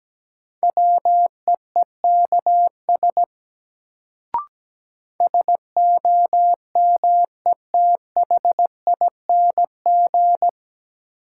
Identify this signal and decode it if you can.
Morse code, heard as a keyed tone, about 17 wpm, 700 Hz. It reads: WEEKS SOMETHING
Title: Sets of 5 Words - New Word Drill - 17wpm